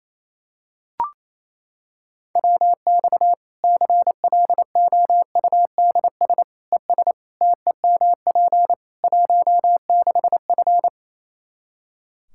Morse code, heard as a keyed tone, about 28 wpm, 700 Hz. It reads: WX CLOUDS ES TEMP 16F
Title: Practice QSO - Exchange 2 - Info - 28wpm